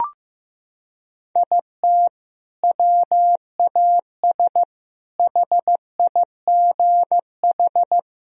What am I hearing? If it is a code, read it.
Morse code, heard as a keyed tone, about 15 wpm, 700 Hz. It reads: IT WAS HIGH
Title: Sentences from Top 1000 Words - Review Part 2 - 15wpm